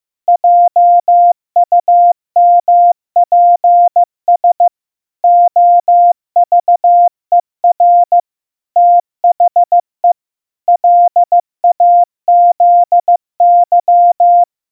Morse code, heard as a keyed tone, about 15 wpm, 700 Hz. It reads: JUMPS OVER THE LAZY
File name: morse-practice-15wpm